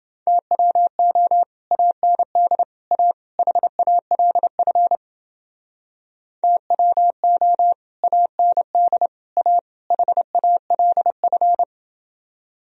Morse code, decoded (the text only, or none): TWO AND A HALF TWO AND A HALF